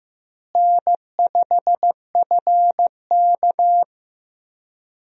Morse code, heard as a keyed tone, about 16 wpm, 700 Hz. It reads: N5FK
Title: USA Call Signs - 4 Characters - Repeated 2x - 15wpm